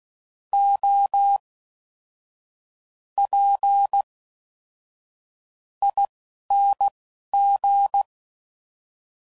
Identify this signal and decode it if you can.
Morse code, heard as a keyed tone, about 16 wpm, spaced as for 11 wpm, 800 Hz. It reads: O P ING